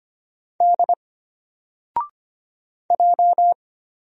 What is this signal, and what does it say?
Morse code, heard as a keyed tone, about 25 wpm, 700 Hz. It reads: D J
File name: Single-Letter-Number-25wpm